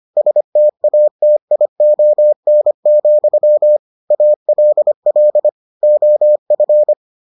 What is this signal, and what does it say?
Morse code, heard as a keyed tone, about 25 wpm, 600 Hz. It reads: STATION, ALL OF